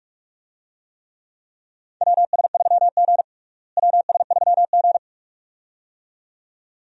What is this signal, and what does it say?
Morse code, heard as a keyed tone, about 46 wpm, 700 Hz. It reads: WS3G WS3G
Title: USA Call Signs - 4 Characters - Repeated 2x - 45wpm